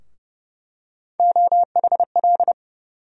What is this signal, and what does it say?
Morse code, heard as a keyed tone, about 30 wpm, 700 Hz. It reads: OHL
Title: USA Call Sign Postfixes - 3 Letters 30wpm